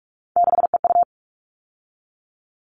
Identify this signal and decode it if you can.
Morse code, heard as a keyed tone, about 44 wpm, 700 Hz. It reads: 6EU
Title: All Cut Numbers - 3-Digit 45wpm